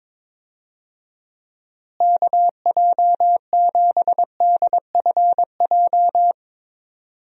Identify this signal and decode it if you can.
Morse code, heard as a keyed tone, about 22 wpm, 700 Hz. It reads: KJ7DFJ
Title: USA Call Signs - 6 Characters - Repeated 2x - 22wpm